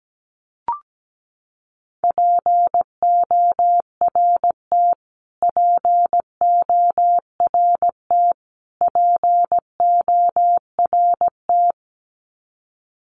Morse code, heard as a keyed tone, about 17 wpm, 700 Hz. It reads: PORT PORT PORT
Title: Top 500 Words Repeated 17wpm